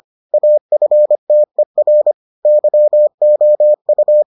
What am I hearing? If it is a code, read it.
Morse code, heard as a keyed tone, about 25 wpm, 600 Hz. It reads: AFTER YOU